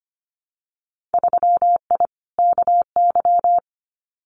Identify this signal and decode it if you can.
Morse code, heard as a keyed tone, about 25 wpm, 700 Hz. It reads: 3I KY
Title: Winter Field Day Exchanges 25wpm